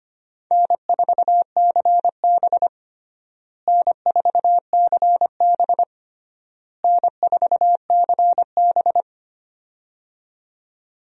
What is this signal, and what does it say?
Morse code, heard as a keyed tone, about 25 wpm, 700 Hz. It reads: N4CB N4CB N4CB
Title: USA Call Signs - 4 Characters - Repeated 3x - 25wpm